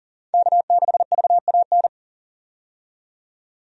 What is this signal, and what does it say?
Morse code, heard as a keyed tone, about 40 wpm, 700 Hz. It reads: K6VAN
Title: USA Call Signs - 5 Characters - 40wpm